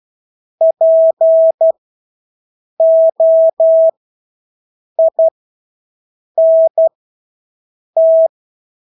Morse code, heard as a keyed tone, about 12 wpm, spaced as for 6 wpm, 650 Hz. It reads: POINT
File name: Engelske ord 12-6 wpm